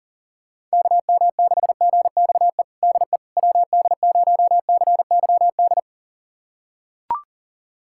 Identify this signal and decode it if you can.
Morse code, heard as a keyed tone, about 40 wpm, 700 Hz. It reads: KM6GXE DE WD0CYD
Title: USA Callsign DE Callsign - 6 Characters - 40wpm